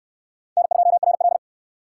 Morse code, heard as a keyed tone, about 68 wpm, 700 Hz. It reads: N2RL